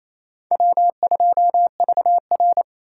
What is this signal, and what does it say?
Morse code, heard as a keyed tone, about 28 wpm, 700 Hz. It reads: W2VR
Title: USA Call Signs - 4 Characters - 4x Character Spacing - 28wpm